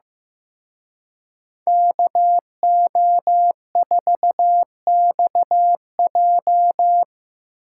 Morse code, heard as a keyed tone, about 18 wpm, 700 Hz. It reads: KO4XJ
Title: Speed Racing - USA 2x2 Call Signs 15wpm